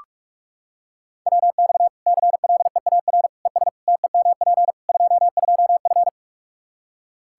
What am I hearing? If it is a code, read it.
Morse code, heard as a keyed tone, about 42 wpm, 700 Hz. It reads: WX CLEAR ES TEMP 22F
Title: Practice QSO - Exchange 2 - Info - 45wpm